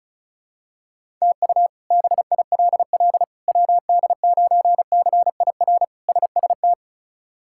Morse code, heard as a keyed tone, about 36 wpm, 700 Hz. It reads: TU BILL WD9CIR SST
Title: K1USN SST QSO Practice - 35wpm